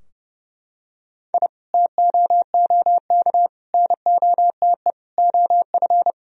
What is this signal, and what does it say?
Morse code, heard as a keyed tone, about 30 wpm, 700 Hz. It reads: I TOOK NOTE OF
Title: Sentences from Top 1000 Words - Review Part 2 - 30wpm